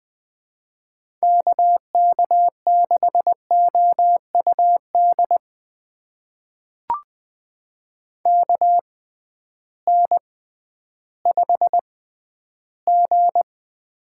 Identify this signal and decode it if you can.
Morse code, heard as a keyed tone, about 20 wpm, spaced as for 12 wpm, 700 Hz. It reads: KK6OUD KN5G
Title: USA Call Signs - 6x Character Spacing - 20wpm